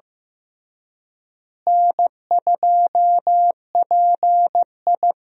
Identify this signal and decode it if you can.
Morse code, heard as a keyed tone, about 14 wpm, 700 Hz. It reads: N2PI
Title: USA Call Signs - 4 Characters - Repeated 3x - 15wpm